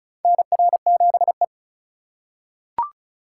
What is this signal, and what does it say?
Morse code, heard as a keyed tone, about 36 wpm, 700 Hz. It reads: NR7E
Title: USA Call Signs - 8x Character Spacing - 35wpm